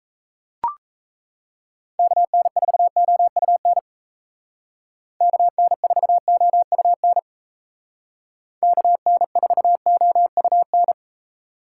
Morse code, heard as a keyed tone, about 37 wpm, 700 Hz. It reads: KN4OUN KN4OUN KN4OUN